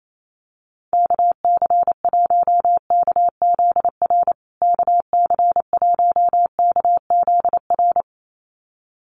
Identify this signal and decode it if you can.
Morse code, heard as a keyed tone, about 28 wpm, 700 Hz. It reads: KC1KZR KC1KZR
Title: USA Call Signs Repeated 2x 28wpm